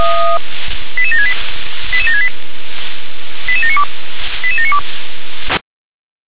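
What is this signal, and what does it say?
Five-tone selective call (ZVEI), heard as quick runs of five tones, tones of about 70 ms. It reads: 88768 88768 88792 88792
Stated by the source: ZVEI five-tone selective call received on an SDR.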